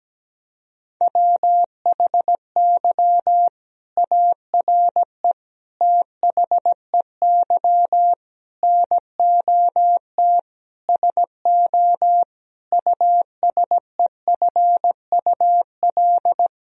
Morse code, heard as a keyed tone, about 17 wpm, 700 Hz. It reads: WHY ARE THEY NOT SO USEFUL
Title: Sentences from Top 1000 Words - Review Part 4 - 17wpm